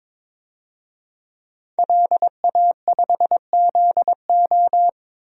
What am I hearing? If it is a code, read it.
Morse code, heard as a keyed tone, about 22 wpm, 700 Hz. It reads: LA5ZO
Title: DXCC-100 Call Signs - 5 Characters - 22wpm